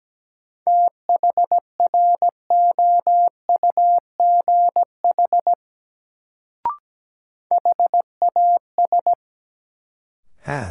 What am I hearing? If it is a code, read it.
Morse code, heard as a keyed tone, about 17 wpm, 700 Hz. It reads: THROUGH HAS
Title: Sets of 2 Words - New Word Drill - 17wpm